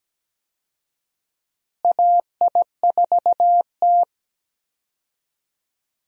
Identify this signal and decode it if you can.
Morse code, heard as a keyed tone, about 17 wpm, 700 Hz. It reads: AI4T
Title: USA Call Signs - 4 Characters - Repeated 3x - 17wpm